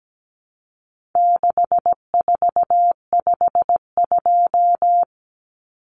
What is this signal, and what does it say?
Morse code, heard as a keyed tone, about 17 wpm, 700 Hz. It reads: 6452
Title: Common Cut Numbers - 4-Digit 17wpm